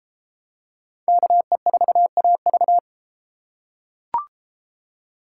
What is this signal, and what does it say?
Morse code, heard as a keyed tone, about 32 wpm, 700 Hz. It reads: KE4AV